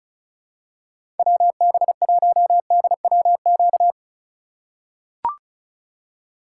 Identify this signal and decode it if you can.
Morse code, heard as a keyed tone, about 36 wpm, 700 Hz. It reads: WB1DWQ
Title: USA Call Signs - 6 Characters - 8x Character Spacing - 35wpm